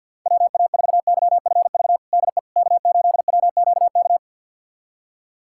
Morse code, heard as a keyed tone, about 53 wpm, 700 Hz. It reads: WA4YFV DE K8WQK
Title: USA Callsign DE Callsign 50wpm